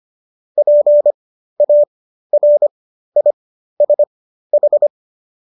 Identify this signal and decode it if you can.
Morse code, heard as a keyed tone, about 25 wpm, spaced as for 13 wpm, 590 Hz. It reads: PARISH